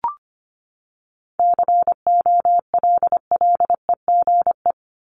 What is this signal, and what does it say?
Morse code, heard as a keyed tone, about 25 wpm, 700 Hz. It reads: COLLEGE
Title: Top 1001-1250 Words 25wpm